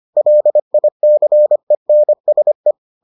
Morse code, heard as a keyed tone, about 25 wpm, 600 Hz. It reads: LICENSE